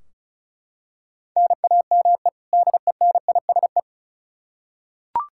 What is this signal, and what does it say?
Morse code, heard as a keyed tone, about 34 wpm, 700 Hz. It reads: NAME DENISE
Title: Practice QSO - Exchange 1 - Beginning - 35wpm